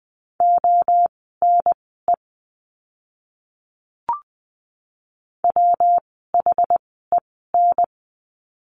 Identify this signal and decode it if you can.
Morse code, heard as a keyed tone, about 20 wpm, spaced as for 14 wpm, 700 Hz. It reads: ONE WHEN
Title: 115 Lesson 19f - Letter w - Words 20wpm